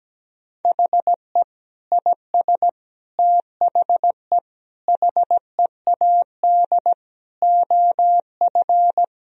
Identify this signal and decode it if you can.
Morse code, heard as a keyed tone, about 17 wpm, 700 Hz. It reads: HE IS THE HEAD OF
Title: Sentences from Top 1000 Words - Review Part 4 - 17wpm